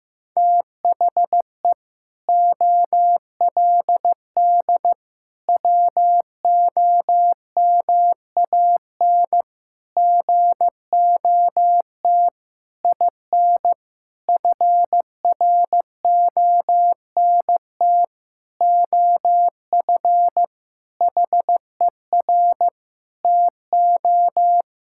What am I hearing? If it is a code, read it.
Morse code, heard as a keyed tone, about 15 wpm, 700 Hz. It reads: THE OLD WOMAN GOT IN FRONT OF HER TO
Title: Sentences from Top 1600 Words 15wpm